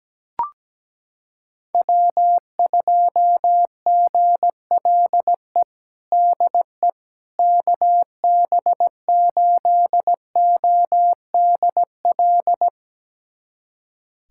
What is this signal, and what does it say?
Morse code, heard as a keyed tone, about 17 wpm, 700 Hz. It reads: W2GLE DE KB8ODL
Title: USA Callsign DE Callsign 17wpm